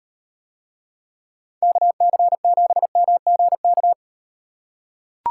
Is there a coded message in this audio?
Yes: Morse code from 1.6 to 3.9 s, heard as a keyed tone, about 38 wpm, 700 Hz. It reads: KC7MGK